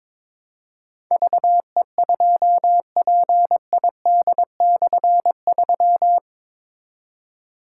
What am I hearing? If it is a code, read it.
Morse code, heard as a keyed tone, about 22 wpm, 700 Hz. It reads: VE2PID/3
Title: K1USN SST QSO Practice - 22wpm